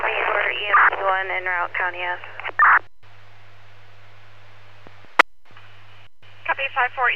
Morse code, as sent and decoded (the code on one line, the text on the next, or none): none